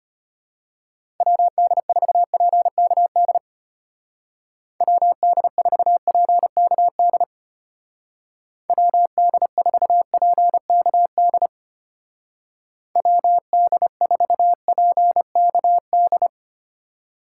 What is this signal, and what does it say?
Morse code, heard as a keyed tone, about 30 wpm, 700 Hz. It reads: WD4PKD WD4PKD WD4PKD WD4PKD
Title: Speed Racing - USA Call Signs 25wpm